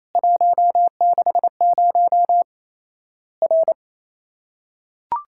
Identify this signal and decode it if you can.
Morse code, heard as a keyed tone, about 28 wpm, 700 Hz. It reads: 160 R
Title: Worked All Europe DX Contest - QTC Exchange - 28wpm